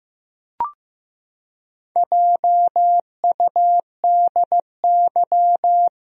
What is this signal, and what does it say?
Morse code, heard as a keyed tone, about 15 wpm, 700 Hz. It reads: JUDY
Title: USA Common Names - 15wpm